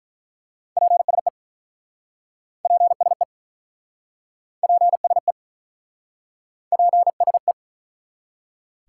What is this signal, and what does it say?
Morse code, heard as a keyed tone, about 47 wpm, 700 Hz. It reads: PSE PSE PSE PSE